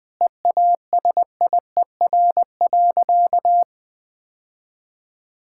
Morse code, heard as a keyed tone, about 20 wpm, 700 Hz. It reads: EASIER.
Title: Sentences with 3-Word Phrases - 20wpm